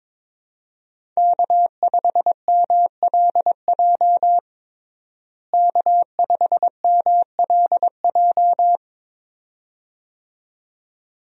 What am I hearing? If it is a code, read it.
Morse code, heard as a keyed tone, about 22 wpm, 700 Hz. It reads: K5MLJ K5MLJ
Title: USA Call Signs - 5 Characters - Repeated 2x - 22wpm